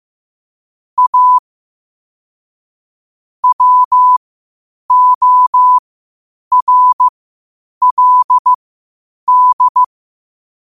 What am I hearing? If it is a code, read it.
Morse code, heard as a keyed tone, about 15 wpm, spaced as for 8 wpm, 1000 Hz. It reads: A WORLD
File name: 8wpm